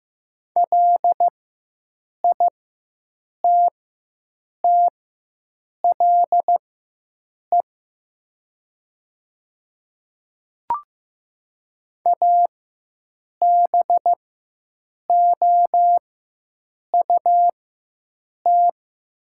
Morse code, heard as a keyed tone, about 15 wpm, spaced as for 7 wpm, 700 Hz. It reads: LITTLE ABOUT
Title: 4x Character Spacing - Top 200 Words - 15wpm